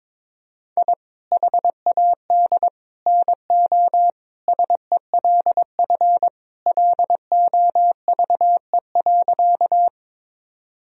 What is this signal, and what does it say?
Morse code, heard as a keyed tone, about 22 wpm, 700 Hz. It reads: I HAD NO SELF LOVE.
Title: Sentences from Top 1000 Words - Review Part 1 - 22wpm